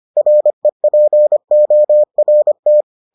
Morse code, heard as a keyed tone, about 25 wpm, 600 Hz. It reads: REPORT